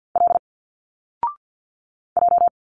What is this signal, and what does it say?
Morse code, heard as a keyed tone, about 50 wpm, 700 Hz. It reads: L J
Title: Single Letters 50wpm